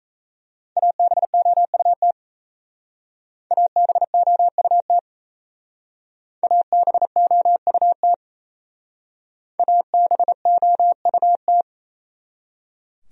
Morse code, heard as a keyed tone, about 35 wpm, 700 Hz. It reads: ABOUT ABOUT ABOUT ABOUT